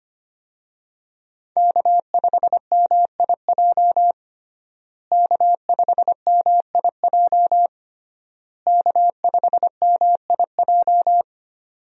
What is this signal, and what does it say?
Morse code, heard as a keyed tone, about 25 wpm, 700 Hz. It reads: K5MIJ K5MIJ K5MIJ
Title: USA Call Signs - 5 Characters - Repeated 3x - 25wpm